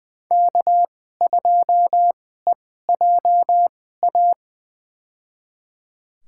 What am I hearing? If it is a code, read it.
Morse code, heard as a keyed tone, about 20 wpm, spaced as for 14 wpm, 700 Hz. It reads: K2EJA
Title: USA Call Signs - 5 Characters - 2x Character Spacing - 20wpm